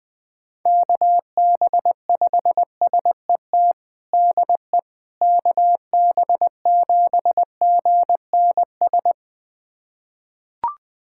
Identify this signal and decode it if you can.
Morse code, heard as a keyed tone, about 20 wpm, 700 Hz. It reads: KB5SET DE KB7GNS
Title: USA Callsign DE Callsign - 6 Characters - 20wpm